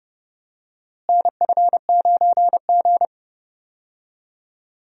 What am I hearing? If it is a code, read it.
Morse code, heard as a keyed tone, about 30 wpm, 700 Hz. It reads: NF9G
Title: USA Call Signs - 4 Characters - Repeated 3x - 30wpm